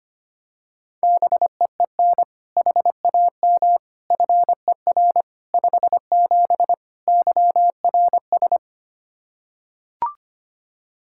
Morse code, heard as a keyed tone, about 25 wpm, 700 Hz. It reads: BEEN HAM FER 57 YRS
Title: K1USN SST QSO Practice 25wpm